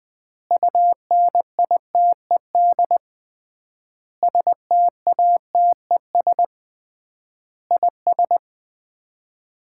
Morse code, heard as a keyed tone, about 20 wpm, 700 Hz. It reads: UNITED STATES IS
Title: 3x Word-Spacing - Sets of 5 Words - 20wpm